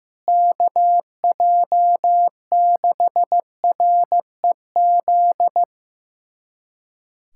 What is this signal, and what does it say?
Morse code, heard as a keyed tone, about 15 wpm, 700 Hz. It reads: KJ6REZ
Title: USA Callsign DE Callsign - 6 Characters - 15wpm